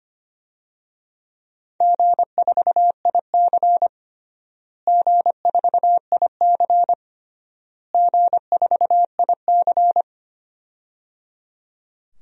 Morse code, heard as a keyed tone, about 25 wpm, 700 Hz. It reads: G4IC G4IC G4IC
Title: DXCC-100 Call Signs - 4 Characters - Repeated 3x - 25wpm